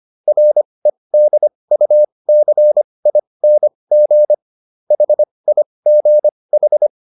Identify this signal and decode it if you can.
Morse code, heard as a keyed tone, about 25 wpm, spaced as for 20 wpm, 600 Hz. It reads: REDUCING HIGH